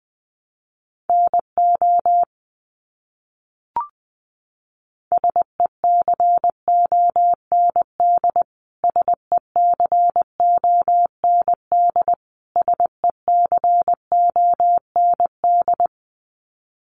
Morse code, heard as a keyed tone, about 20 wpm, 700 Hz. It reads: NO SECOND SECOND SECOND
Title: Top 500 Words Repeated 20wpm